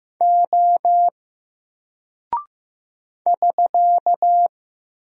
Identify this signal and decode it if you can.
Morse code, heard as a keyed tone, about 15 wpm, 700 Hz. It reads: O <SK>
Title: Single Character (Letter:Number:Punctuation:Prosign) Alt 15wpm